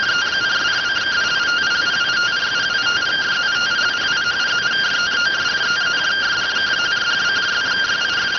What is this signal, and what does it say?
Two-tone teleprinter signal, about 170 Hz shift.